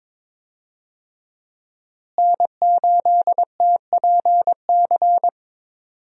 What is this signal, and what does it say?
Morse code, heard as a keyed tone, about 22 wpm, 700 Hz. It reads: N8TPC
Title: USA Call Signs - 5 Characters - Repeated 2x - 22wpm